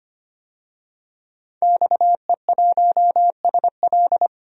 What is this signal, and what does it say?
Morse code, heard as a keyed tone, about 25 wpm, 700 Hz. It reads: XE1SL
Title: DXCC-100 Call Signs - 5 Characters - Repeated 3x - 25wpm